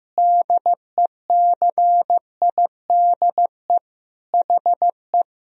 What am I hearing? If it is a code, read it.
Morse code, heard as a keyed tone, about 15 wpm, 700 Hz. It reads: DECIDE HE
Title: Sentences from Top 500 Words and Abbreviations 15wpm